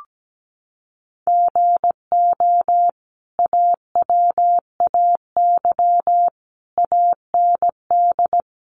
Morse code, heard as a keyed tone, about 17 wpm, 700 Hz. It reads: GO AWAY AND
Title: Sentences from Top 1300 Words 17wpm